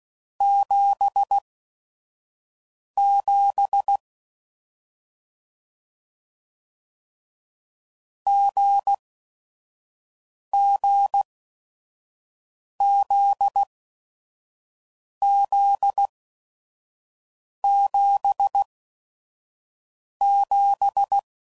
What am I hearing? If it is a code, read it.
Morse code, heard as a keyed tone, about 16 wpm, spaced as for 5 wpm, 800 Hz. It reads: 77 GGZZ77